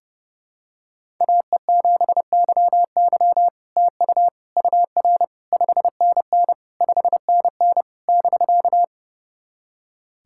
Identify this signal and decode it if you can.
Morse code, heard as a keyed tone, about 31 wpm, 700 Hz. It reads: AE7YY TU UR 5NN 5NN <BK>
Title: SOTA QSO Practice 30wpm